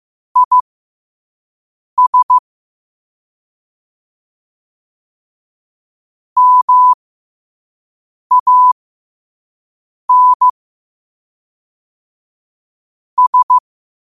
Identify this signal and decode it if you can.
Morse code, heard as a keyed tone, about 15 wpm, spaced as for 5 wpm, 1000 Hz. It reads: IS MAN S
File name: qotdfiles / 5wpm